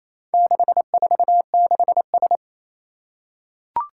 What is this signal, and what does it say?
Morse code, heard as a keyed tone, about 28 wpm, 700 Hz. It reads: 646S
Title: Four Challenging Letter-Number Combinations 28wpm